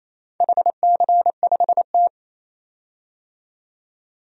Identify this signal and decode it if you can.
Morse code, heard as a keyed tone, about 29 wpm, 700 Hz. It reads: HC5T
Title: DXCC-100 Call Signs - 4 Characters - 28wpm